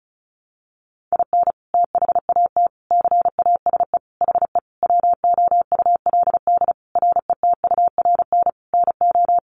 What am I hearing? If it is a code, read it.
Morse code, heard as a keyed tone, about 34 wpm, 700 Hz. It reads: IN THAT CASE HE WOULD RETURN NO
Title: Sentences from Top 800 Words 35wpm